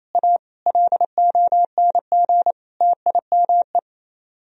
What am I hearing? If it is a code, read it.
Morse code, heard as a keyed tone, about 35 wpm, 700 Hz. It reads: A LONG TIME
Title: Speed-Racing - Sets of 4 Words 28wpm